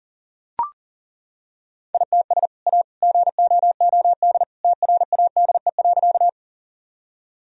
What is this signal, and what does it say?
Morse code, heard as a keyed tone, about 40 wpm, 700 Hz. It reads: ITS A GOOD TRADE.
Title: Sentences from Top 1000 Words - Review Part 4 - 40wpm